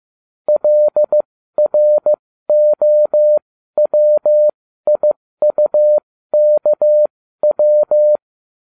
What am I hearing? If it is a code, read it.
Morse code, heard as a keyed tone, about 15 wpm, spaced as for 13 wpm, 600 Hz. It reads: LROWIUKW
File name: cw / ubacw13 2